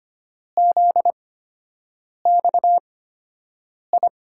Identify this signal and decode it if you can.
Morse code, heard as a keyed tone, about 25 wpm, spaced as for 7 wpm, 700 Hz. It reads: ZXI
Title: USA Call Signs - 5 Characters - 8x Character Spacing - 25wpm